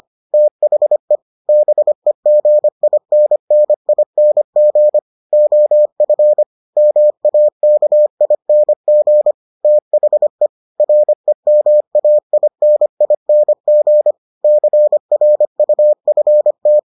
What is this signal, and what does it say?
Morse code, heard as a keyed tone, about 25 wpm, 600 Hz. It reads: THE BEGINNING OF MAKING THE REMAINING CRUFT